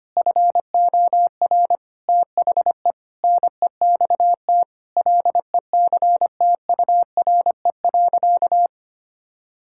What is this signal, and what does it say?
Morse code, heard as a keyed tone, about 25 wpm, 700 Hz. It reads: FOR THE NEXT LECTURE.
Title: Sentences from Top 1600 Words 25wpm